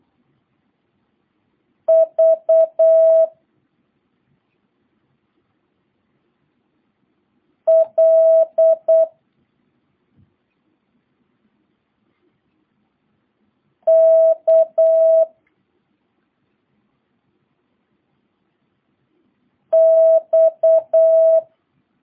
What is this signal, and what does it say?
Morse code, heard as a keyed tone, about 8 wpm, 650 Hz. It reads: V L K X